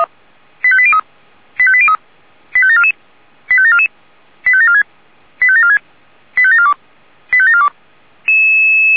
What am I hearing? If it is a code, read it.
Five-tone selective call (ZVEI), heard as quick runs of five tones, tones of about 70 ms. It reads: 85803 85803 85744 85744 85746 85746 85742 85742
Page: SDR/ZVEI